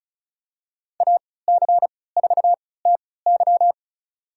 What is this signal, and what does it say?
Morse code, heard as a keyed tone, about 36 wpm, spaced as for 20 wpm, 690 Hz. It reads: AC4TY
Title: USA Call Signs - 5 Characters - 3x Character Spacing - 35wpm